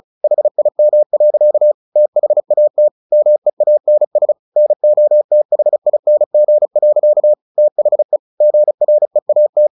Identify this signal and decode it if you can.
Morse code, heard as a keyed tone, about 34 wpm, 600 Hz. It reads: HIM. THAT MEANS NOTHING. THE GREAT
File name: the old man and the sea / the old man and the sea 1 35wpm